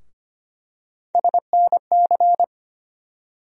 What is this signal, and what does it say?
Morse code, heard as a keyed tone, about 25 wpm, 700 Hz. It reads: SNC
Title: USA Call Sign Postfixes - 3 Letters 25wpm